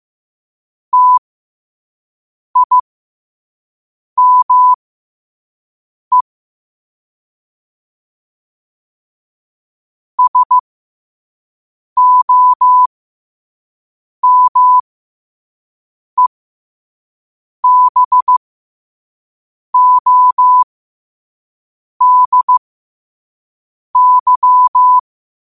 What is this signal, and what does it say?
Morse code, heard as a keyed tone, about 15 wpm, spaced as for 5 wpm, 1000 Hz. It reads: TIME SOMEBODY
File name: qotdfiles / 5wpm